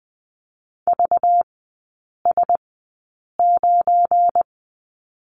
Morse code, heard as a keyed tone, about 20 wpm, spaced as for 8 wpm, 690 Hz. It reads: VS9
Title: 191 Lesson 31e - Letter v - Three Characters 20wpm